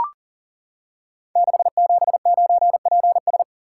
Morse code, heard as a keyed tone, about 42 wpm, 700 Hz. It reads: 679PS
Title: Code Groups - 5 Letter:Numbers - 40wpm